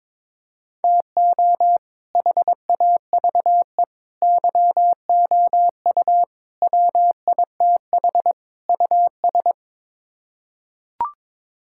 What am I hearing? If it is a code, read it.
Morse code, heard as a keyed tone, about 22 wpm, 700 Hz. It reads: TO HAVE YOU WITH US
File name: practice / 22wpm / Build A Phrase - 5-Word Phrases - 22wpm